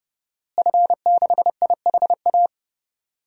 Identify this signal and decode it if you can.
Morse code, heard as a keyed tone, about 30 wpm, 700 Hz. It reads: F6IHA
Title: DXCC-100 Call Signs - 5 Characters - Repeated 3x - 30wpm